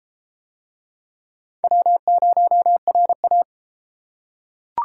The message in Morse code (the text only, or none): W0RA